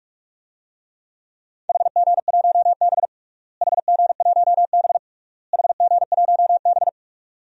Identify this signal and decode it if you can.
Morse code, heard as a keyed tone, about 44 wpm, 700 Hz. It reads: HG1B HG1B HG1B
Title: DXCC-100 Call Signs - 4 Characters - Repeated 3x - 45wpm